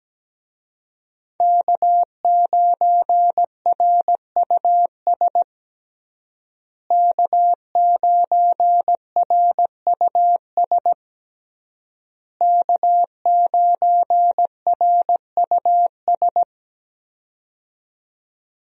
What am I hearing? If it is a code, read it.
Morse code, heard as a keyed tone, about 17 wpm, 700 Hz. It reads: K9RUS K9RUS K9RUS
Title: USA Call Signs - 5 Characters - Repeated 3x - 17wpm